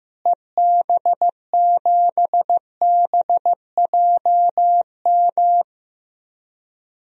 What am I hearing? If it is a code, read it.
Morse code, heard as a keyed tone, about 15 wpm, 700 Hz. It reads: EB7BJM
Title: DXCC-100 Call Signs - 6 Characters - Repeated 3x - 15wpm